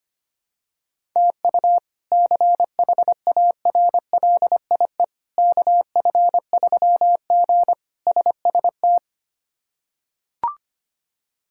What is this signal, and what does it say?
Morse code, heard as a keyed tone, about 25 wpm, 700 Hz. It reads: TU CHARLIE KF3G SST
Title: K1USN SST QSO Practice - 25wpm